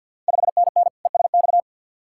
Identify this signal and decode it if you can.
Morse code, heard as a keyed tone, about 52 wpm, 700 Hz. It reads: 5NN ESX
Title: Ontario QSO Party Exchange 50wpm